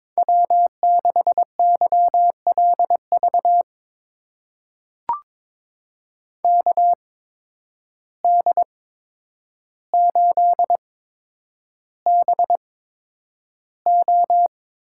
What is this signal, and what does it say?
Morse code, heard as a keyed tone, about 22 wpm, spaced as for 9 wpm, 700 Hz. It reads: W6YLV KD8BO
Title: USA Call Signs - 8x Character Spacing - 22wpm